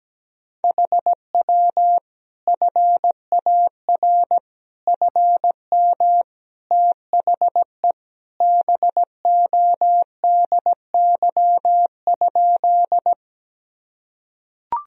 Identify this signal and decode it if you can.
Morse code, heard as a keyed tone, about 17 wpm, 700 Hz. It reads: HW FAR FM THE BODY?
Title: Sentences from Top 400 Words and Abbreviations 17wpm